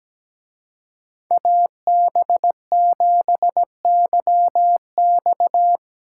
Morse code, heard as a keyed tone, about 19 wpm, 700 Hz. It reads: AB7YX
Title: Speed Racing - USA 2x2 Call Signs 17wpm